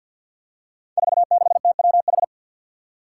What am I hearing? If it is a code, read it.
Morse code, heard as a keyed tone, about 49 wpm, 700 Hz. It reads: 46TWH
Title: Five Challenging Letter-Number Combinations 50wpm